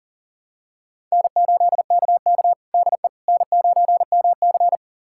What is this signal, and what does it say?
Morse code, heard as a keyed tone, about 40 wpm, 700 Hz. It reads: N8KK DE N9MC
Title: USA Callsign DE Callsign - 4 Characters - 40wpm